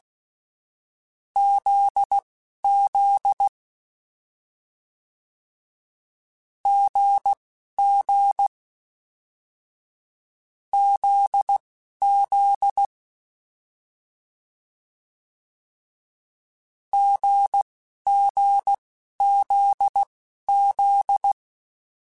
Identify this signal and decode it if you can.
Morse code, heard as a keyed tone, about 16 wpm, spaced as for 11 wpm, 800 Hz. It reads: ZZ GG ZZ GGZZ
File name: Difficult Combos / Difficult Combos-mp3s / GZ GZ7 @10